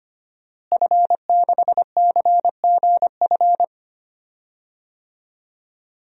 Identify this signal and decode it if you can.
Morse code, heard as a keyed tone, about 25 wpm, 700 Hz. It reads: F6CGF
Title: DXCC-100 Call Signs - 5 Characters - Repeated 2x - 25wpm